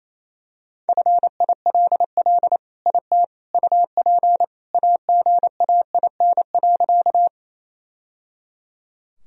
Morse code, heard as a keyed tone, about 28 wpm, 700 Hz. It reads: FILL IT UP AGAIN.
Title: Sentences from Top 1000 Words - Review Part 3 - 28wpm